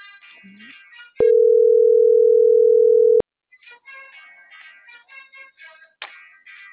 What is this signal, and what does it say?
Ringback tone (North America), 440 + 480 Hz.